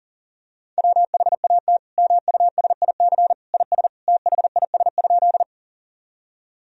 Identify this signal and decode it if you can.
Morse code, heard as a keyed tone, about 40 wpm, 700 Hz. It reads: WHAT MUSIC IS THIS?
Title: Sentences from Top 1000 Words - Review Part 1 - 40wpm